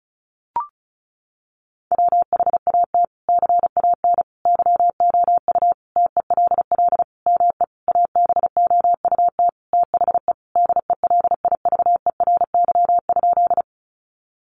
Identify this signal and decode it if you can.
Morse code, heard as a keyed tone, about 34 wpm, 700 Hz. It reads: WHAT CAN YOU TELL ME ABOUT THE DELIVERY?
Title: Sentences from Top 1500 Words 35wpm